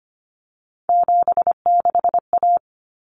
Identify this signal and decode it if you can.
Morse code, heard as a keyed tone, about 25 wpm, 700 Hz. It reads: 76A
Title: Common Cut Numbers - 3-Digit 25wpm